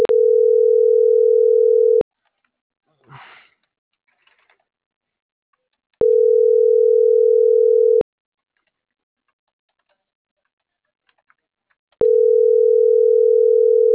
Ringback tone (North America), 440 + 480 Hz.